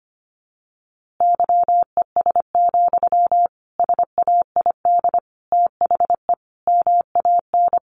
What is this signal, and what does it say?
Morse code, heard as a keyed tone, about 25 wpm, 700 Hz. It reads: YES, SAID THE MAN
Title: A Sound Of Thunder 25wpm